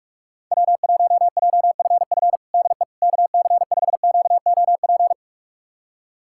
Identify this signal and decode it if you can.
Morse code, heard as a keyed tone, about 41 wpm, 700 Hz. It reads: W1JFF DE KC5QOP
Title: USA Callsign DE Callsign 45wpm